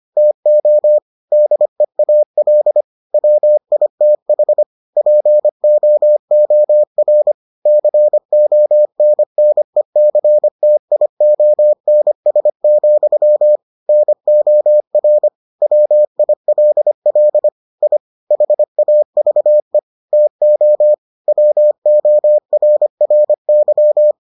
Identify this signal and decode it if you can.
Morse code, heard as a keyed tone, about 25 wpm, 600 Hz. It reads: TO DEAL WITH POOR CONNECTIONS, NOR WILL I HAVE TO WORRY